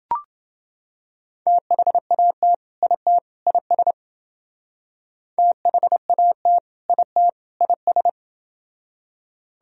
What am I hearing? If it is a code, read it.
Morse code, heard as a keyed tone, about 28 wpm, 700 Hz. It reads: THAT IT IS THAT IT IS